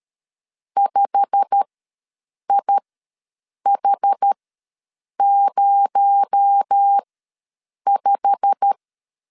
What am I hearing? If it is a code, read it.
Morse code, heard as a keyed tone, about 13 wpm, spaced as for 7 wpm, 800 Hz. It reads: 5IH05
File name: LEZIONI / 8 YJ45 / 8 YJ45 ws 12wpm 3146